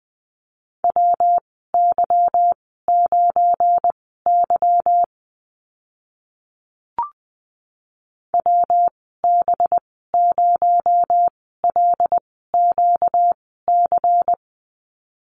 Morse code, heard as a keyed tone, about 20 wpm, spaced as for 15 wpm, 700 Hz. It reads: WY9Y WB0LQC
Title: 238 Lesson 38g - Letter q - Call Signs 20wpm